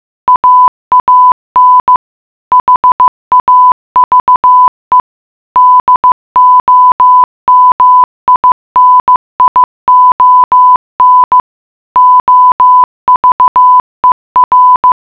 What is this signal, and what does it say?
Morse code, heard as a keyed tone, about 15 wpm, 1000 Hz. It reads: AAN HAVE DOMINION OVER
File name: mutation.15wpm